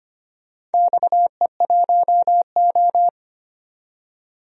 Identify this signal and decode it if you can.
Morse code, heard as a keyed tone, about 25 wpm, 700 Hz. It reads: XE1O
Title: DXCC-100 Call Signs - 4 Characters - 25wpm